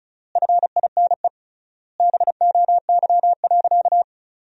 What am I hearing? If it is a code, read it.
Morse code, heard as a keyed tone, about 34 wpm, 700 Hz. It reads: FINE BOY.
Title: 3x Word-Spacing - Sentences from Top 500 Words - 35wpm